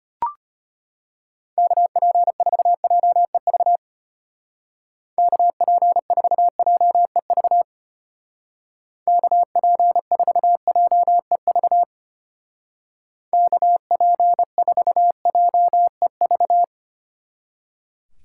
Morse code, heard as a keyed tone, about 31 wpm, 700 Hz. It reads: KP4JEV KP4JEV KP4JEV KP4JEV
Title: Speed Racing - USA 2x3 Call Signs 25wpm